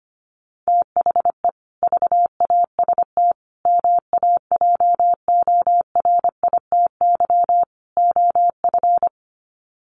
Morse code, heard as a keyed tone, about 25 wpm, 700 Hz. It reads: THE VAST MAJORITY OF
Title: Sets of 4 Words 25wpm